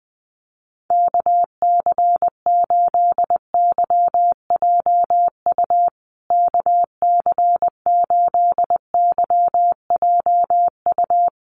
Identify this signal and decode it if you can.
Morse code, heard as a keyed tone, about 20 wpm, 700 Hz. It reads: KC8YJU KC8YJU
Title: USA Call Signs Repeated 3x 20wpm